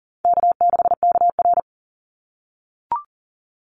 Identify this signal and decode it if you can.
Morse code, heard as a keyed tone, about 38 wpm, 700 Hz. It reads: K6KR
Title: Easy Call Signs - 40wpm